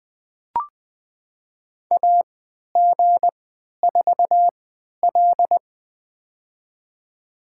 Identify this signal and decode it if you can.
Morse code, heard as a keyed tone, about 20 wpm, spaced as for 11 wpm, 700 Hz. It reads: AG4L
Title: USA Call Signs - 4 Characters - 3x Character Spacing - 20wpm